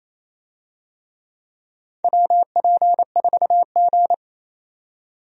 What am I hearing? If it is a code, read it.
Morse code, heard as a keyed tone, about 28 wpm, 700 Hz. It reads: WP4G
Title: USA Call Signs - 4 Characters - 28wpm